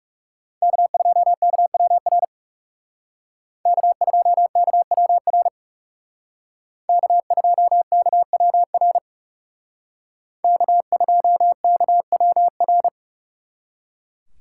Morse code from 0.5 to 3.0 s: K2KWR

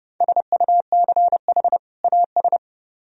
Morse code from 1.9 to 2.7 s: AS

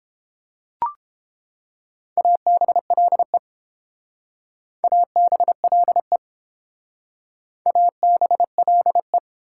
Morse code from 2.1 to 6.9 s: ABLE ABLE

